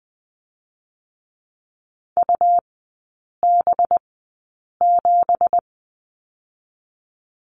Morse code, heard as a keyed tone, about 20 wpm, spaced as for 8 wpm, 700 Hz. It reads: UB7
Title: 218 Lesson 35e - Number 0 - Three Characters 20wpm